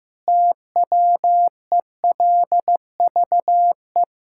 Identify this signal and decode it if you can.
Morse code, heard as a keyed tone, about 15 wpm, 700 Hz. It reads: TWELVE
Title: Pangram Challenge v2 - 15wpm